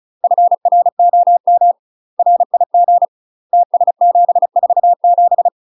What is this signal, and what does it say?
Morse code, heard as a keyed tone, about 36 wpm, 700 Hz. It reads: FROM RIG TS747
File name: UK MASTER QSOs TRAINING / 35WPM / UK MASTER QSOs 0016